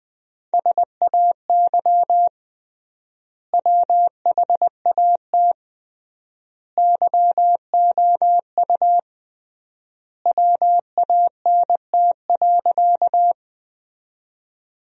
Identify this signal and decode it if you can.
Morse code, heard as a keyed tone, about 20 wpm, 700 Hz. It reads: SAY WHAT YOU WANT.
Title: 3x Word-Spacing - Sentences from Top 200 Words - 20wpm